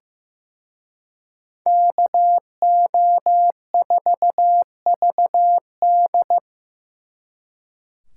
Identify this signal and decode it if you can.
Morse code, heard as a keyed tone, about 18 wpm, 700 Hz. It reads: KO4VD
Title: Speed Racing - USA 2x2 Call Signs 15wpm